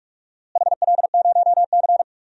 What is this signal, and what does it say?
Morse code, heard as a keyed tone, about 43 wpm, 700 Hz. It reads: HL0C
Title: DXCC-100 Call Signs - 4 Characters - Repeated 3x - 45wpm